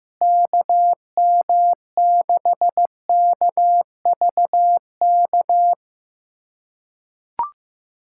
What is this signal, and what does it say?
Morse code, heard as a keyed tone, about 15 wpm, 700 Hz. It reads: KM6KVK
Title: USA Callsign DE Callsign - 6 Characters - 15wpm